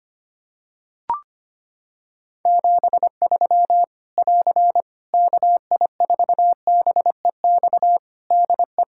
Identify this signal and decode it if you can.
Morse code, heard as a keyed tone, about 25 wpm, 700 Hz. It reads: 73 <AR> KI4BEX DE
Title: Practice QSO - Exchange 4 - Ending - 25wpm